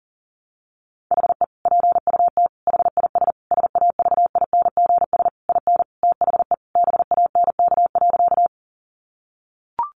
Morse code, heard as a keyed tone, about 37 wpm, 700 Hz. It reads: HE PUT HIS SAVINGS IN THE BANK.
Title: Sentences from Top 1200 Words 40wpm